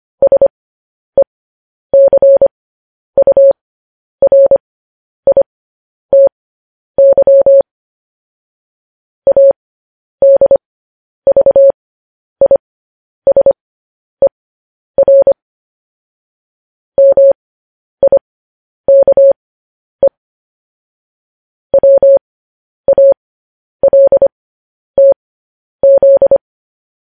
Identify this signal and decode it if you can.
Morse code, heard as a keyed tone, about 25 wpm, spaced as for 10 wpm, 550 Hz. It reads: SECURITY ADVISER MIKE WALTZ
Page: news-headlines-in-morse-code-at-10-wpm